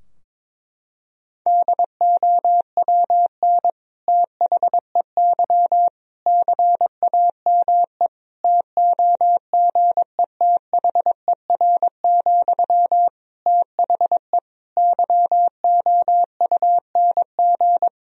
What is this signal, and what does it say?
Morse code, heard as a keyed tone, about 22 wpm, 700 Hz. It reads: DOWN THEY CAME TOGETHER, THE YOUNG
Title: Sentences from Top 1000 Words - Review Part 2 - 22wpm